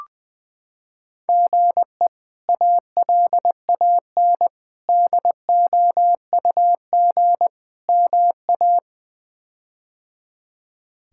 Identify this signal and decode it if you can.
Morse code, heard as a keyed tone, about 20 wpm, 700 Hz. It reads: GE ALAN DOUG MA
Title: K1USN SST QSO Practice - 20wpm